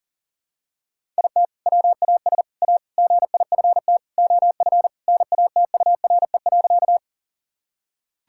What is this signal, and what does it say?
Morse code, heard as a keyed tone, about 40 wpm, 700 Hz. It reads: IT WAS A GIFT OF NATURE.
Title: Sentences from Top 1000 Words - Review Part 6 - 40wpm